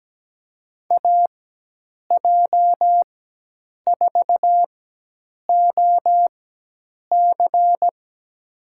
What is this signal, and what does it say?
Morse code, heard as a keyed tone, about 17 wpm, spaced as for 8 wpm, 700 Hz. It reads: AJ4OC
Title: USA Call Signs - 5 Characters - 4x Character Spacing - 17wpm